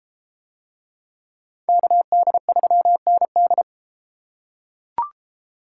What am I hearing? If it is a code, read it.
Morse code, heard as a keyed tone, about 32 wpm, 700 Hz. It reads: KD3ND